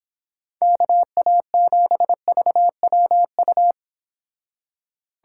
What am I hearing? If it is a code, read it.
Morse code, heard as a keyed tone, about 26 wpm, 700 Hz. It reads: KA7VWU